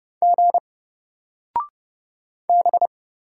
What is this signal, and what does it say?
Morse code, heard as a keyed tone, about 30 wpm, 700 Hz. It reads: G B
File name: Single-Letter-Number-30wpm